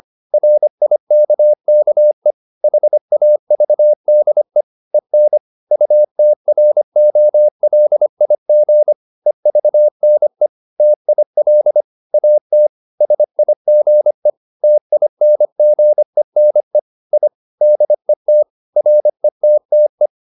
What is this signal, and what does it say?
Morse code, heard as a keyed tone, about 25 wpm, 600 Hz. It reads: RIKKE HAVDE EN UTROLIG EVNE TIL AT SIGE TINGENE I DET RETTE
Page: Ferie 25 wpm